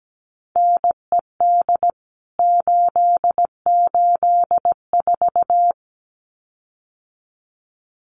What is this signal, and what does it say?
Morse code, heard as a keyed tone, about 17 wpm, 700 Hz. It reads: NED 884
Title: CWT Exchange 17wpm